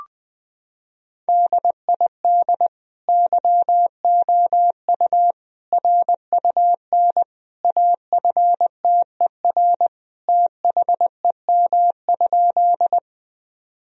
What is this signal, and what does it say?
Morse code, heard as a keyed tone, about 21 wpm, 700 Hz. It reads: DID YOU RUN AFTER THEM?
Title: Sentences from Top 1000 Words - Review Part 1 - 20wpm